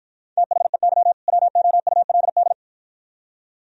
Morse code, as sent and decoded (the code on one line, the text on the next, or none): - .... -.-- / .-- --- ..- .-.. -..
THEY WOULD